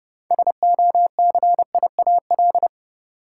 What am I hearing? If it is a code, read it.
Morse code, heard as a keyed tone, about 32 wpm, 700 Hz. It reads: SOCIAL
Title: K1USN SST QSO Practice 30wpm